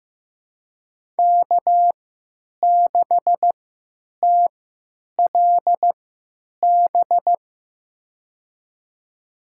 Morse code, heard as a keyed tone, about 15 wpm, spaced as for 8 wpm, 700 Hz. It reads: K6TLB
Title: USA Call Signs - 5 Characters - 3x Character Spacing - 15wpm